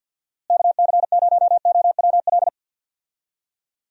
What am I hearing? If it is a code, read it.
Morse code, heard as a keyed tone, about 50 wpm, 700 Hz. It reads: KC0OWL
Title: USA Call Signs - 6 Characters - Repeated 3x - 50wpm